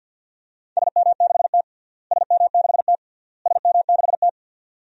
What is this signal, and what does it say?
Morse code, heard as a keyed tone, about 52 wpm, 700 Hz. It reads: SM6T SM6T SM6T
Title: DXCC-100 Call Signs - 4 Characters - Repeated 3x - 50wpm